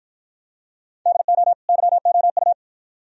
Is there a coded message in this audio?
Yes: Morse code from 1.1 to 2.5 s, heard as a keyed tone, about 50 wpm, 700 Hz. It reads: DO YOU